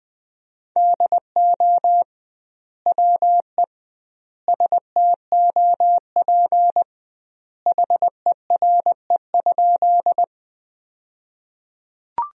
Morse code, heard as a keyed tone, about 21 wpm, 700 Hz. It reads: DO WE STOP HERE?
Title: 2x Word-Spacing - Sentences from Top 300 Words - 20wpm